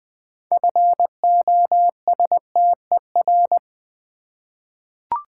Morse code, heard as a keyed tone, about 20 wpm, 700 Hz. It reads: FOSTER
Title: 6 Letter Words - First 3 Letters Given - 20wpm